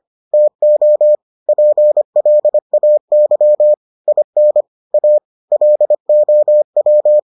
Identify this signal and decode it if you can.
Morse code, heard as a keyed tone, about 25 wpm, 600 Hz. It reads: TO PLAY IN A LOW